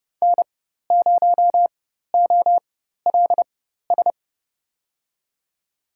Morse code, heard as a keyed tone, about 30 wpm, spaced as for 14 wpm, 700 Hz. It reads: N0OLS
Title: USA Call Signs - 4x Character Spacing - 30wpm